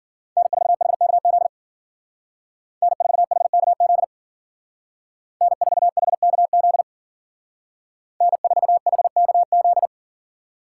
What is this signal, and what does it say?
Morse code, heard as a keyed tone, about 52 wpm, 700 Hz. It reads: N4HKZ N4HKZ N4HKZ N4HKZ